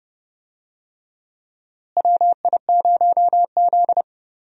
Morse code, heard as a keyed tone, about 31 wpm, 700 Hz. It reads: WI0Z
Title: USA Call Signs - 4 Characters - Repeated 3x - 30wpm